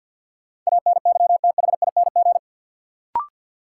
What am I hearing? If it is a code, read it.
Morse code, heard as a keyed tone, about 52 wpm, 700 Hz. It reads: ANYTHING
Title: Sets of 4 Words - New Word Drill - 50wpm